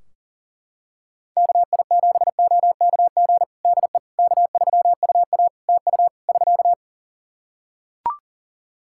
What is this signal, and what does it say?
Morse code, heard as a keyed tone, about 40 wpm, 700 Hz. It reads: KI7OKG DE K3UA TU <SK>
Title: Practice QSO - Exchange 4 - Ending - 40wpm